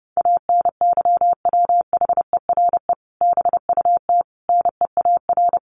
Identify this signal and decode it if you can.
Morse code, heard as a keyed tone, about 30 wpm, 700 Hz. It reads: ANYWHERE BUT NEAR
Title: Sentences from Top 1400 Words 30wpm